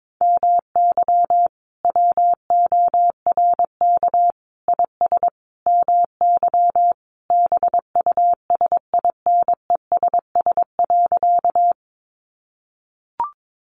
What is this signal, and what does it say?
Morse code, heard as a keyed tone, about 22 wpm, 700 Hz. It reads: MY WORK IS MY BUSINESS.
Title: Sentences from Top 600 Words 22wpm